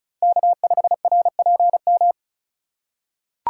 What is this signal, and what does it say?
Morse code, heard as a keyed tone, about 36 wpm, 700 Hz. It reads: K5RPM
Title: USA Call Signs - 5 Characters - 4x Character Spacing - 35wpm